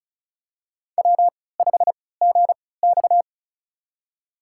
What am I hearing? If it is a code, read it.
Morse code, heard as a keyed tone, about 36 wpm, spaced as for 20 wpm, 700 Hz. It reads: W5GX
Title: USA Call Signs - 4 Characters - 3x Character Spacing - 35wpm